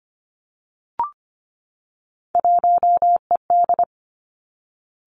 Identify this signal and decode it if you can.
Morse code, heard as a keyed tone, about 25 wpm, 700 Hz. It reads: 1ED
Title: Three Challenging Letter-Number Combinations 25wpm